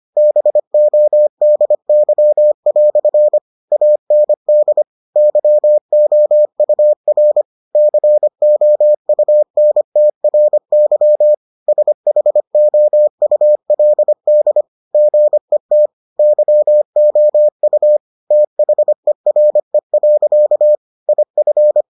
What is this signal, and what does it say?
Morse code, heard as a keyed tone, about 25 wpm, 600 Hz. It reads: BODY" AND YOUR COUNTRY SHOULD GET YOU THERE. IF